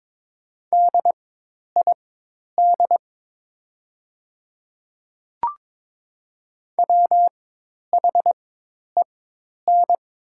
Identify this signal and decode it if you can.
Morse code, heard as a keyed tone, about 22 wpm, spaced as for 10 wpm, 700 Hz. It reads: DID WHEN
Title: 4x Character Spacing - Top 500 Words - 22wpm